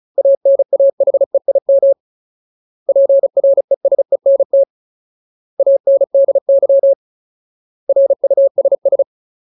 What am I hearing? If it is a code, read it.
Morse code, heard as a keyed tone, about 34 wpm, 550 Hz. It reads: ANAHEIM PRESENT ANDY RUSS
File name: Ultimate Word List / 35WPM Ultimate